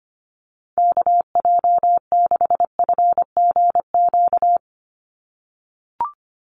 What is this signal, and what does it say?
Morse code, heard as a keyed tone, about 25 wpm, 700 Hz. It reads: KJ6FGQ
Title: USA Call Signs 25wpm